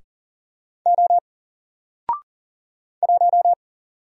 Morse code, heard as a keyed tone, about 40 wpm, 700 Hz. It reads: O 1
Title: Single Character (Letter:Number:Punctuation:Prosign) Alt 40wpm